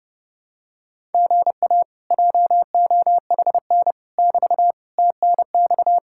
Morse code, heard as a keyed tone, about 29 wpm, 700 Hz. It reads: GA JOHN <BT> TNX